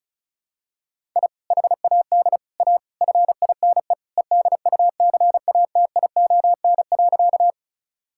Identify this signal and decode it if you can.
Morse code, heard as a keyed tone, about 36 wpm, 700 Hz. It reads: I HAD A FINE EDUCATION.
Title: Sentences from Top 1000 Words - Review Part 6 - 35wpm